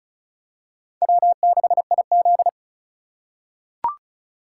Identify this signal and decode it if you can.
Morse code, heard as a keyed tone, about 36 wpm, 690 Hz. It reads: W6IZ
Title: USA Call Signs - 4 Characters - 3x Character Spacing - 35wpm